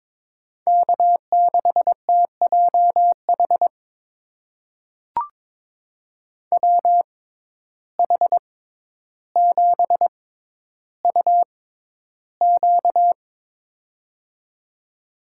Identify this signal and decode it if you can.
Morse code, heard as a keyed tone, about 22 wpm, spaced as for 11 wpm, 700 Hz. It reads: K6TJH WH7UQ
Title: USA Call Signs - 6x Character Spacing - 22wpm